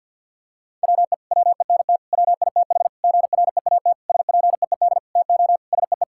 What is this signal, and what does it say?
Morse code, heard as a keyed tone, about 50 wpm, 700 Hz. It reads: WE WENT WITH GREAT SPEED TO SEE
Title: Sentences from Top 1000 Words - Review Part 5 - 50wpm